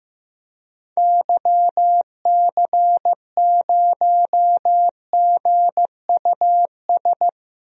Morse code, heard as a keyed tone, about 16 wpm, 700 Hz. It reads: YC0GUS
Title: DXCC-100 Call Signs - 6 Characters - Repeated 3x - 15wpm